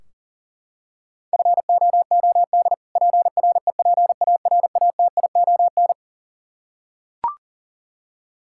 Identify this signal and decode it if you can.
Morse code, heard as a keyed tone, about 40 wpm, 700 Hz. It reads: FOOD PREPARATION
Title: Practice QSO - Exchange 3 - More Info - 40wpm